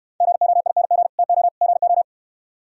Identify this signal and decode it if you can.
Morse code, heard as a keyed tone, about 68 wpm, 700 Hz. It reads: GOING TO GO